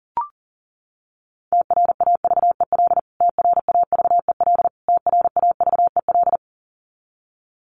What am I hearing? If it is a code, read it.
Morse code, heard as a keyed tone, about 40 wpm, 700 Hz. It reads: TRAVEL TRAVEL TRAVEL
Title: Top 500 Words Repeated 40wpm